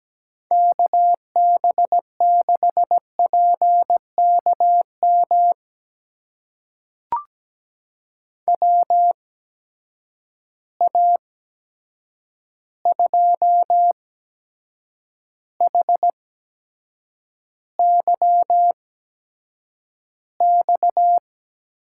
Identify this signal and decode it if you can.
Morse code, heard as a keyed tone, about 17 wpm, spaced as for 7 wpm, 700 Hz. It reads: KB6PKM WA2HYX
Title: USA Call Signs - 6 Characters - 8x Character Spacing - 17wpm